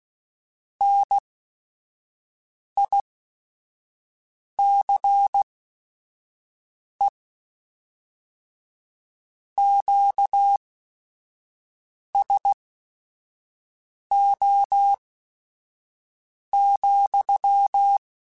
Morse code, heard as a keyed tone, about 16 wpm, spaced as for 5 wpm, 800 Hz. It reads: NICE QSO,